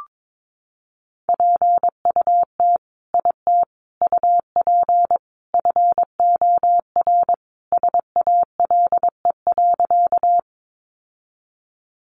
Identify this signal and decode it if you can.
Morse code, heard as a keyed tone, about 22 wpm, 700 Hz. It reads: PUT IT UP FOR SALE.
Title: Sentences from Top 1400 Words 22wpm